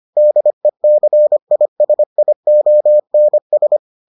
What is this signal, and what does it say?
Morse code, heard as a keyed tone, about 25 wpm, 600 Hz. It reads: DECISIONS